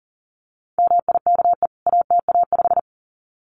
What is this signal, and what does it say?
Morse code, heard as a keyed tone, about 40 wpm, 700 Hz. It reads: MIKE ATA5
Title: CWT Exchange 40wpm